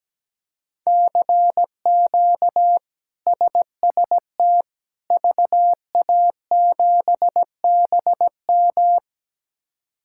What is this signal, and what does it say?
Morse code, heard as a keyed tone, about 17 wpm, 700 Hz. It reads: CQ SST VA7BM
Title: K1USN SST QSO Practice - 17wpm